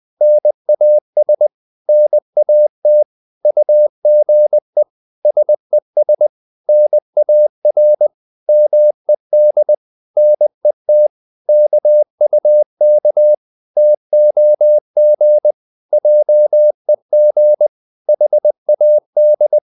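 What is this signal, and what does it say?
Morse code, heard as a keyed tone, about 20 wpm, 600 Hz. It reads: NAS NAT UGE SES NAR MED NET KUK TOG JEG HAD
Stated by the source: Morse code practice of short Danish words at 20 wpm, from CW med Gnister.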